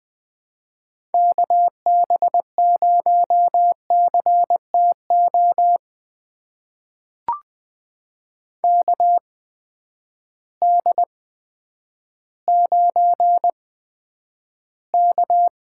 Morse code, heard as a keyed tone, about 20 wpm, spaced as for 10 wpm, 700 Hz. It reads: KB0CTO KD9K
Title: USA Call Signs - 6 Characters - 8x Character Spacing - 20wpm